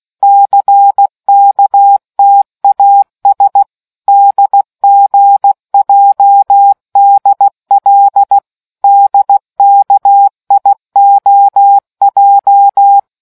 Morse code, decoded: CKTAS DGJDL DKIOJ

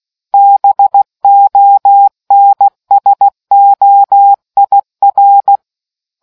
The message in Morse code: BONSOIR

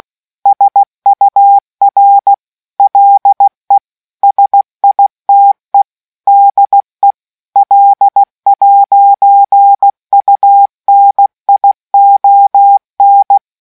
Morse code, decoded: SUR LE SITE DE L'UNION

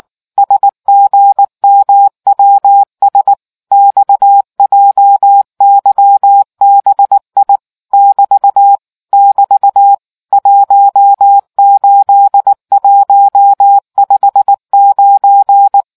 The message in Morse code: SGMWS XJYBI <BT> <BT> 18159

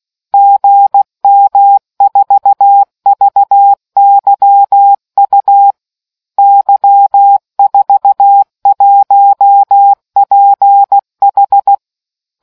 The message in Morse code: GM4VYU Y41PH

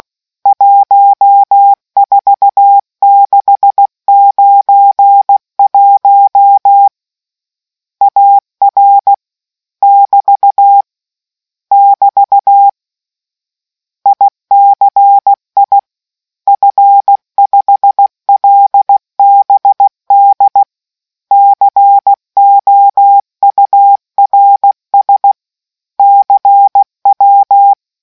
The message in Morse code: 14691 AR <BT> <BT> ICI F5LBD COURS CW

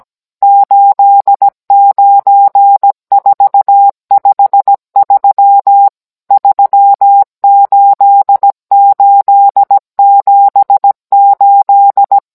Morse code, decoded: 89453 38878

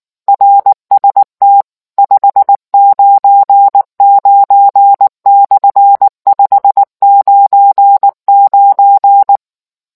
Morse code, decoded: RST 599/599

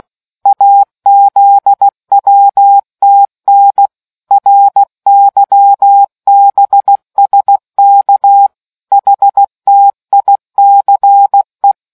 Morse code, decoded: AZWTN RYBSK HTICE

